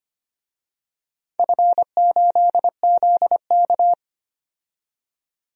Morse code, heard as a keyed tone, about 25 wpm, 700 Hz. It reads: F8ZK
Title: DXCC-100 Call Signs - 4 Characters - 25wpm